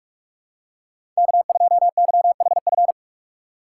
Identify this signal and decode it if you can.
Morse code, heard as a keyed tone, about 41 wpm, 700 Hz. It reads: K2YHF
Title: USA Call Signs - 5 Characters - 4x Character Spacing - 45wpm